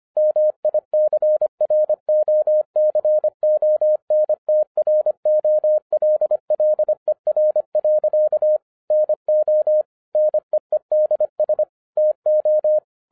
Morse code, heard as a keyed tone, about 25 wpm, 610 Hz. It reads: MICROCONTROLLER. NO NEEDS TO
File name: 2011-10-mp3 / n09-CW TO PS2 CONVERTER 25wpm